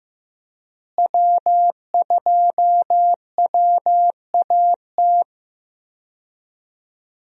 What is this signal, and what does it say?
Morse code, heard as a keyed tone, about 16 wpm, 700 Hz. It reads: W2WAT
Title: USA Call Signs - 5 Characters - Repeated 3x - 15wpm